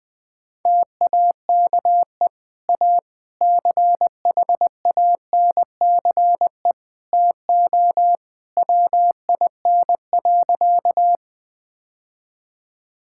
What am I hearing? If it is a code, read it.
Morse code, heard as a keyed tone, about 20 wpm, 700 Hz. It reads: TAKE A CHANCE TO WIN.
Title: Sentences with 3-Word Phrases - Spoken Beginning - 20wpm